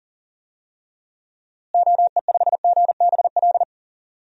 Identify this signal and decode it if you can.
Morse code, heard as a keyed tone, about 40 wpm, 700 Hz. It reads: OE5GBL
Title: DXCC-100 Call Signs - 6 Characters - Repeated 2x - 40wpm